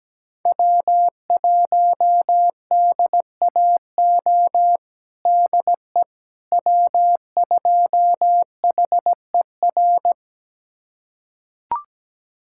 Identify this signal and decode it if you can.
Morse code, heard as a keyed tone, about 17 wpm, 700 Hz. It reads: W1DAO DE W2HER
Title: USA Callsign DE Callsign - 5 Characters - 17wpm